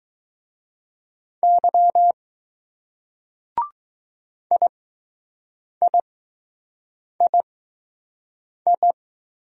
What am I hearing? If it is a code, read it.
Morse code, heard as a keyed tone, about 20 wpm, spaced as for 6 wpm, 700 Hz. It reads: Y IIII